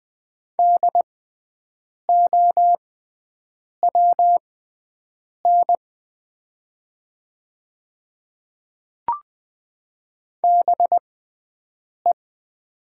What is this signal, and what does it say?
Morse code, heard as a keyed tone, about 20 wpm, spaced as for 7 wpm, 700 Hz. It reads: DOWN BE
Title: 6x Character Spacing - Top 100 Words - 20wpm